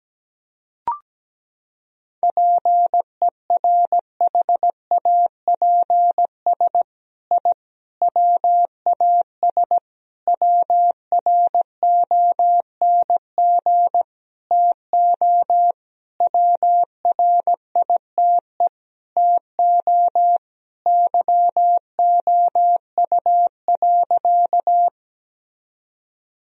Morse code, heard as a keyed tone, about 17 wpm, 700 Hz. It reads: PERHAPS I WAS WRONG TO WRITE TO YOU.
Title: Sentences from Top 1000 Words - Review Part 1 - 17wpm